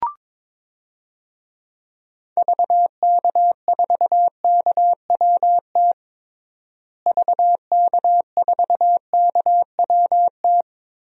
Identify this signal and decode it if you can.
Morse code, heard as a keyed tone, about 22 wpm, 700 Hz. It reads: VK4KWT VK4KWT
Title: DXCC-100 Call Signs - 6 Characters - Repeated 2x - 22wpm